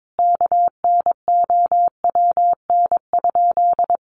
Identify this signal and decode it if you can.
Morse code, heard as a keyed tone, about 22 wpm, 700 Hz. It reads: KNOWN?
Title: Sentences from Top 1300 Words 22wpm